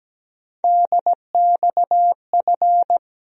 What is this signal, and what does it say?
Morse code, heard as a keyed tone, about 17 wpm, 700 Hz. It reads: DXF
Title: USA Call Sign Postfixes - 3 Letters 17wpm